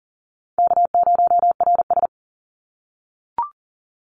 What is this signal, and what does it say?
Morse code, heard as a keyed tone, about 40 wpm, 700 Hz. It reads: K0RS
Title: NA SOTA Chaser Callsigns 40wpm